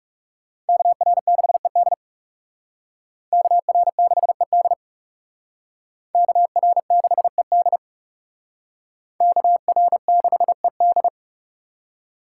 Morse code, heard as a keyed tone, about 37 wpm, 700 Hz. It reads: KR6ED KR6ED KR6ED KR6ED
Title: Speed Racing - USA 2x2 Call Signs 30wpm